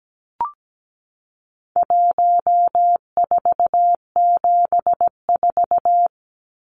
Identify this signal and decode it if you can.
Morse code, heard as a keyed tone, about 17 wpm, 700 Hz. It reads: 1474
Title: Numbers - 4-Digit 17wpm